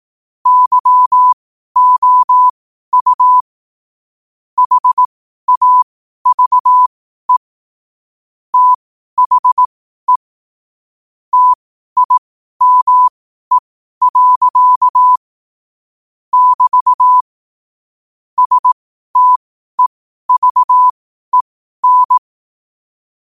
Morse code, heard as a keyed tone, about 18 wpm, spaced as for 12 wpm, 1000 Hz. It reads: YOU HAVE THE TIME. <BT> STEVEN
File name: qotdfiles / 12wpm